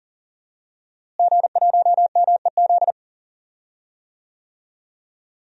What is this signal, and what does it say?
Morse code, heard as a keyed tone, about 37 wpm, 700 Hz. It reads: G1MEZ
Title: DXCC-100 Call Signs - 5 Characters - Repeated 2x - 40wpm